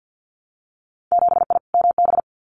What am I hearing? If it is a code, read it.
Morse code, heard as a keyed tone, about 50 wpm, 700 Hz. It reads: 7I MB
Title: Winter Field Day Exchanges Encore 50wpm